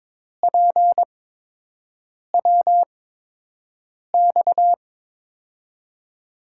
Morse code, heard as a keyed tone, about 22 wpm, spaced as for 6 wpm, 700 Hz. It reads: PWX
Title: USA Call Signs - 6 Characters - 8x Character Spacing - 22wpm